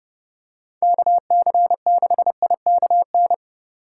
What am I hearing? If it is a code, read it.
Morse code, heard as a keyed tone, about 29 wpm, 700 Hz. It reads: KC6IKN